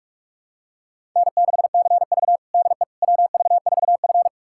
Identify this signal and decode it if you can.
Morse code, heard as a keyed tone, about 46 wpm, 700 Hz. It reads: N6CV DE WV4F
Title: USA Callsign DE Callsign - 4 Characters - 45wpm